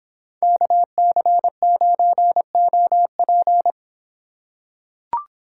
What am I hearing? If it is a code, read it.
Morse code, heard as a keyed tone, about 23 wpm, 700 Hz. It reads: KC9OP